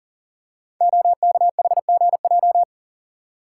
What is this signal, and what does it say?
Morse code, heard as a keyed tone, about 40 wpm, 690 Hz. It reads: OKHGJ
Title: Code Groups - 5 Letter:Numbers - 40wpm